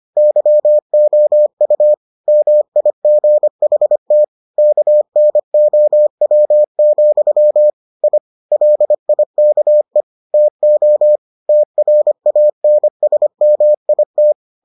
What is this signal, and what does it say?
Morse code, heard as a keyed tone, about 25 wpm, 600 Hz. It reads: YOU MIGHT KNOW, I LIKE TO TRANSMIT